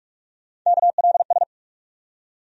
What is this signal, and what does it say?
Morse code, heard as a keyed tone, about 42 wpm, 700 Hz. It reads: KLS
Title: USA Call Sign Postfixes - 3 Letters 45wpm